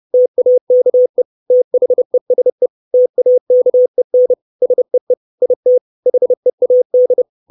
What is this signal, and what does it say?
Morse code, heard as a keyed tone, about 30 wpm, 500 Hz. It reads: TAKE THESE TAKEN SEE IT HEAD
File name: 632 words / 632 Words 30WPM